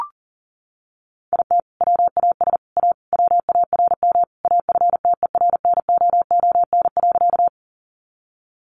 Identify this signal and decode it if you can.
Morse code, heard as a keyed tone, about 40 wpm, 700 Hz. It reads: IT WAS A WARM AFTERNOON.
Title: Sentences from Top 1500 Words 40wpm